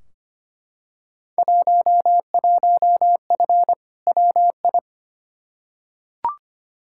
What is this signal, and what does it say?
Morse code, heard as a keyed tone, about 25 wpm, 700 Hz. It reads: 11F WI
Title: Field Day Exchanges Encore - 25wpm